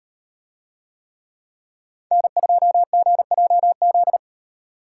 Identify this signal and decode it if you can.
Morse code, heard as a keyed tone, about 38 wpm, 700 Hz. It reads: N2GJZ